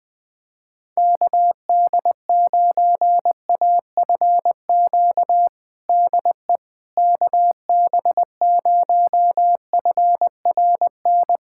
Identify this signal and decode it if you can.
Morse code, heard as a keyed tone, about 20 wpm, 700 Hz. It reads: KD9AFQ DE KB0FRN
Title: USA Callsign DE Callsign - 6 Characters - 20wpm